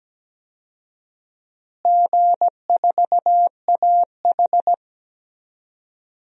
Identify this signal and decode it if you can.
Morse code, heard as a keyed tone, about 17 wpm, 700 Hz. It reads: G4AH
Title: DXCC-100 Call Signs - 4 Characters - Repeated 3x - 17wpm